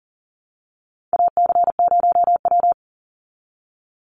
Morse code, heard as a keyed tone, about 40 wpm, 700 Hz. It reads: AC0W
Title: Call Signs Repeated 2x - 40wpm